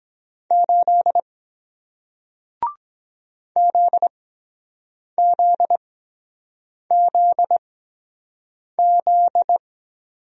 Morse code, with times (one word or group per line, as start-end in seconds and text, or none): 0.5-1.2 8
3.6-9.6 ZZZZ